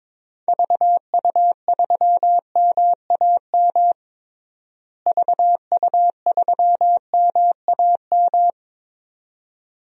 Morse code, heard as a keyed tone, about 22 wpm, 700 Hz. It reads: VU3MAM VU3MAM
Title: DXCC-100 Call Signs - 6 Characters - Repeated 2x - 22wpm